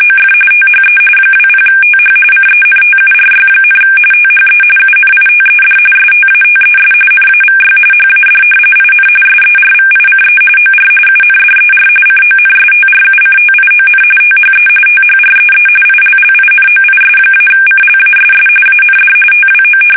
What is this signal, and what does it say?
RTTY, about 45.45 baud. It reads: OF THE 20TH CENTURY, THE UNITED STATES NAVY HAD IN SERVICE OR UNDER CONSTRUCTION THE THREE ILLINOIS-CLASS AND TWO
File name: fsk850 / battleships